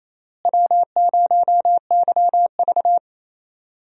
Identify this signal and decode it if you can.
Morse code, heard as a keyed tone, about 27 wpm, 700 Hz. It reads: W0YV
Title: USA Call Signs - 4 Characters - Repeated 2x - 28wpm